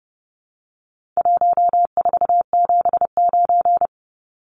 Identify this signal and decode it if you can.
Morse code, heard as a keyed tone, about 30 wpm, 700 Hz. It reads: 1479
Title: Numbers - Contest Style - 4-Digit 30wpm